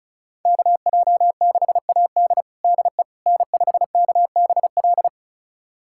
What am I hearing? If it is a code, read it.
Morse code, heard as a keyed tone, about 36 wpm, 700 Hz. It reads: KJ6AD DE N5KBL
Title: USA Callsign DE Callsign - 5 Characters - 35wpm